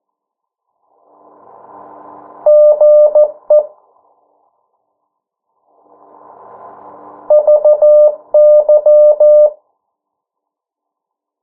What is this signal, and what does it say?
Morse code, heard as a keyed tone, about 14 wpm, 600 Hz. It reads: GE VY
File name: morse mp3 / 09 urok 8